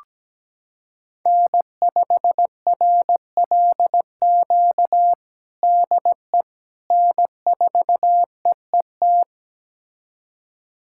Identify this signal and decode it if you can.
Morse code, heard as a keyed tone, about 17 wpm, 700 Hz. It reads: N5RLQ DE N4EET
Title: USA Callsign DE Callsign - 5 Characters - 17wpm